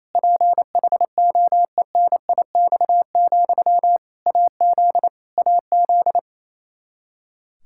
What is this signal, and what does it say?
Morse code, heard as a keyed tone, about 28 wpm, 700 Hz. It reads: PHOENIX, AZ AZ
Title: Practice QSO - Exchange 1 - Beginning - 28wpm